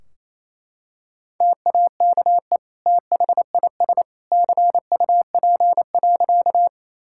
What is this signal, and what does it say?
Morse code, heard as a keyed tone, about 28 wpm, 700 Hz. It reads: TAKE THIS CUP.
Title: Sentences from Top 1600 Words 28wpm